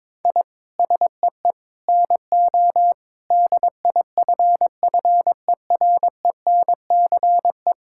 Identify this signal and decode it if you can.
Morse code, heard as a keyed tone, about 22 wpm, 700 Hz. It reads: I SEE NO DIFFERENCE
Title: Sentences from Top 1000 Words - Review Part 5 - 22wpm